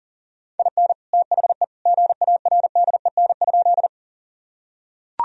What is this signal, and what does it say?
Morse code, heard as a keyed tone, about 40 wpm, 700 Hz. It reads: IN THE GARDEN?
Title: Sentences from Top 1000 Words - Review Part 4 - 40wpm